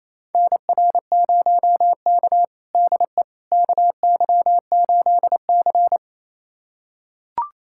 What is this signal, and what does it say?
Morse code, heard as a keyed tone, about 28 wpm, 700 Hz. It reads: NR0K DE KY8C
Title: USA Callsign DE Callsign - 4 Characters - 28wpm